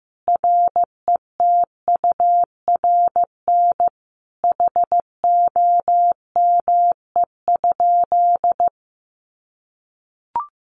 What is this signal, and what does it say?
Morse code, heard as a keyed tone, about 15 wpm, 700 Hz. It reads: RETURN HOME?
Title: Sentences from Top 800 Words 15wpm